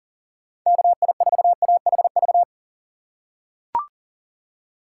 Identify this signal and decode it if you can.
Morse code, heard as a keyed tone, about 40 wpm, 700 Hz. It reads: KI4AHV
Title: USA Call Signs - 2x Character Spacing - 40wpm